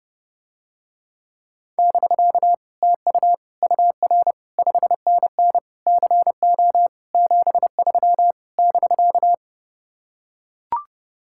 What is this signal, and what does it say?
Morse code, heard as a keyed tone, about 32 wpm, 700 Hz. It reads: <BK> TU UR 5NN CO 73 <BK>